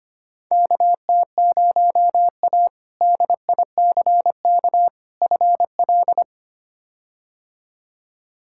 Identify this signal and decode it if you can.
Morse code, heard as a keyed tone, about 25 wpm, 700 Hz. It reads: KT0A DICK FL
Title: K1USN SST QSO Practice - 25wpm